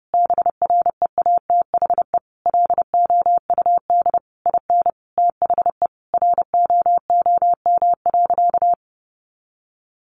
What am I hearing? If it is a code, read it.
Morse code, heard as a keyed tone, about 30 wpm, 700 Hz. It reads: BREATHE LOUD IN THE ROOM.
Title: A Sound Of Thunder 30wpm